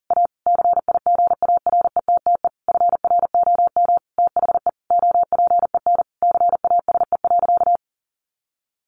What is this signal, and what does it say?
Morse code, heard as a keyed tone, about 40 wpm, 700 Hz. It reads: A CIGARETTE FROM THE OPEN CASE.
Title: Sentences from Top 1000 Words 40wpm